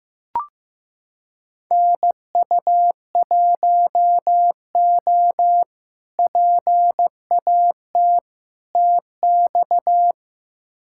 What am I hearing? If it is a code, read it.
Morse code, heard as a keyed tone, about 15 wpm, 700 Hz. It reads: NU1O PAT TX
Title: K1USN SST QSO Practice - 15wpm